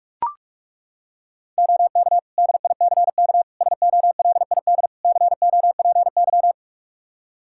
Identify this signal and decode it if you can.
Morse code, heard as a keyed tone, about 42 wpm, 700 Hz. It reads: OK DICK SOLID COPY
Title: Practice QSO - Exchange 3 - More Info - 45wpm